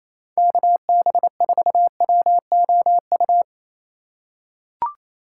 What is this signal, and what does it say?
Morse code, heard as a keyed tone, about 28 wpm, 700 Hz. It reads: KB4WOU
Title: USA Call Signs - 4x Character Spacing - 28wpm